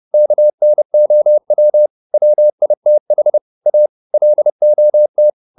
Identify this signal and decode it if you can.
Morse code, heard as a keyed tone, about 30 wpm, 600 Hz. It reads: KNOW WITH A LOT